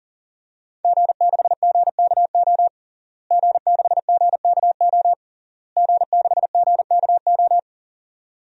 Morse code, heard as a keyed tone, about 40 wpm, 700 Hz. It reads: G6GKO G6GKO G6GKO
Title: DXCC-100 Call Signs - 5 Characters - Repeated 3x - 40wpm